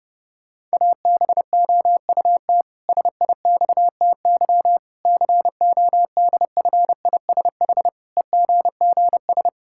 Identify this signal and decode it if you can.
Morse code, heard as a keyed tone, about 30 wpm, 700 Hz. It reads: ABOUT SIXTY CODFISH EGGS
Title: Pangram Challenge v2 - 30wpm